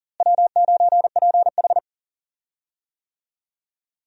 Morse code, heard as a keyed tone, about 37 wpm, 700 Hz. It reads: W9PH
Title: USA Call Signs - 4 Characters - Repeated 3x - 40wpm